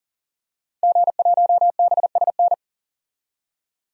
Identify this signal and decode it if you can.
Morse code, heard as a keyed tone, about 40 wpm, 700 Hz. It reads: G1BSN
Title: DXCC-100 Call Signs - 5 Characters - 40wpm